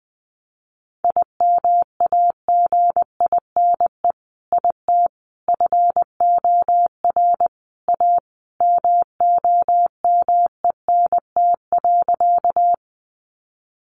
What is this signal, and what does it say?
Morse code, heard as a keyed tone, about 20 wpm, 700 Hz. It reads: IMAGINE IT FOR A MOMENT.
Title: Sentences from Top 1400 Words 20wpm